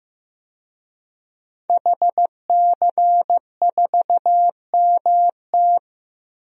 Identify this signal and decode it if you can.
Morse code, heard as a keyed tone, about 15 wpm, 700 Hz. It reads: HC4MT
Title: DXCC-100 Call Signs - 5 Characters - 15wpm